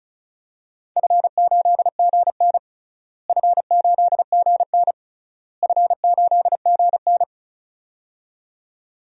Morse code, heard as a keyed tone, about 36 wpm, 700 Hz. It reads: F8GN F8GN F8GN
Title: DXCC-100 Call Signs - 4 Characters - Repeated 3x - 35wpm